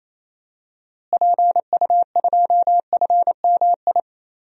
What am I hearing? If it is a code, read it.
Morse code, heard as a keyed tone, about 28 wpm, 700 Hz. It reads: PU2FMI
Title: DXCC-100 Call Signs - 6 Characters - Repeated 3x - 28wpm